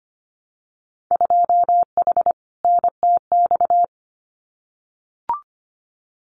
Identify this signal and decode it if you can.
Morse code, heard as a keyed tone, about 25 wpm, 700 Hz. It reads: 2H NTX
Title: Winter Field Day Exchanges Encore 25wpm